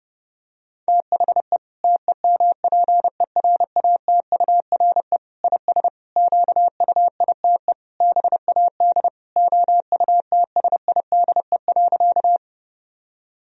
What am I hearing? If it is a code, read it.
Morse code, heard as a keyed tone, about 30 wpm, 700 Hz. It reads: THE TEMPERATURE IS QUITE BAD OUTSIDE.
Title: Sentences from Top 1000 Words - Review Part 4 - 30wpm